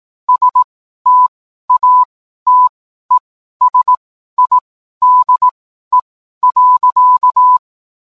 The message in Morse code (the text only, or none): STATESIDE.